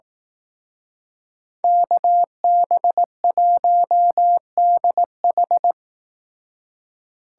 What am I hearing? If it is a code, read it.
Morse code, heard as a keyed tone, about 18 wpm, 700 Hz. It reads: KB1DH
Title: Speed Racing - USA 2x2 Call Signs 15wpm